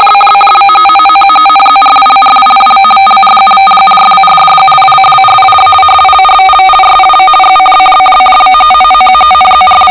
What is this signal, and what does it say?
Two-tone teleprinter signal, about 450 Hz shift.
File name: Globo Condor LW2DTZ RTTY 50baud 7bit 450Hz